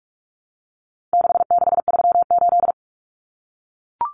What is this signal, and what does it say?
Morse code, heard as a keyed tone, about 44 wpm, 700 Hz. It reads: 6638
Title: Numbers - 4-Digit 45wpm